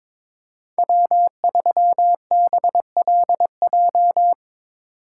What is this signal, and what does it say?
Morse code, heard as a keyed tone, about 22 wpm, 700 Hz. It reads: W3BLJ
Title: USA Call Signs - 5 Characters - 2x Character Spacing - 22wpm